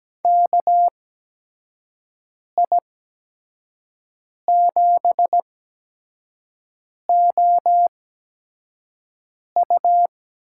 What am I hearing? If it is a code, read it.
Morse code, heard as a keyed tone, about 17 wpm, spaced as for 5 wpm, 700 Hz. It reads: KI7OU
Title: USA Call Signs - 6 Characters - 8x Character Spacing - 17wpm